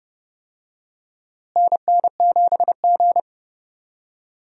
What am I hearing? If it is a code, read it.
Morse code, heard as a keyed tone, about 30 wpm, 700 Hz. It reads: NN7G
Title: USA Call Signs - 4 Characters - Repeated 2x - 30wpm